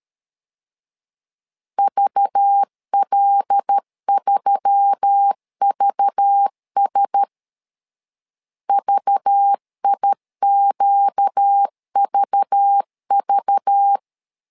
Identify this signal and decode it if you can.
Morse code, heard as a keyed tone, about 13 wpm, 800 Hz. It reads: VL3VS VIQVV
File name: LEZIONI / 5 LBVC / 5 LBV 12wpm 1855